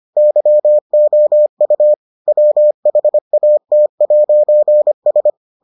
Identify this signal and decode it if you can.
Morse code, heard as a keyed tone, about 25 wpm, 600 Hz. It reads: YOU WHAT'S